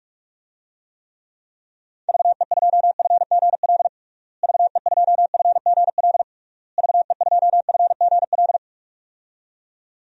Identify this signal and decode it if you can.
Morse code, heard as a keyed tone, about 41 wpm, 700 Hz. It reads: VE2FGL VE2FGL VE2FGL
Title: DXCC-100 Call Signs - 6 Characters - Repeated 3x - 45wpm